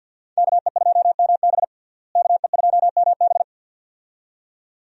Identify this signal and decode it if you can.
Morse code, heard as a keyed tone, about 51 wpm, 700 Hz. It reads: KE2MB KE2MB
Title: USA Call Signs - 5 Characters - Repeated 2x - 50wpm